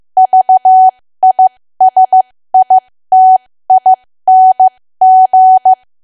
Morse code, heard as a keyed tone, about 15 wpm, spaced as for 13 wpm, 750 Hz. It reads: VISITING
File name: cwa / W1AW-Level2 / 13WPM